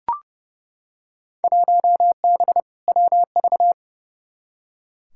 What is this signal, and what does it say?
Morse code, heard as a keyed tone, about 30 wpm, 700 Hz. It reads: 1B WV
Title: Field Day Exchanges - 30wpm